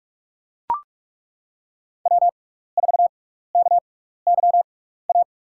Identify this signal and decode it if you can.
Morse code, heard as a keyed tone, about 42 wpm, 700 Hz. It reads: W 4 K Y A
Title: USA Call Signs - 5 Characters - 6x Character Spacing - 45wpm